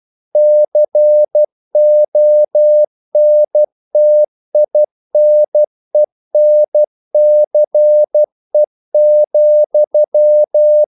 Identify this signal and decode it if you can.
Morse code, heard as a keyed tone, about 12 wpm, 600 Hz. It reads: CONTINENCE,
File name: CW / Practice / EnglishQuotations0112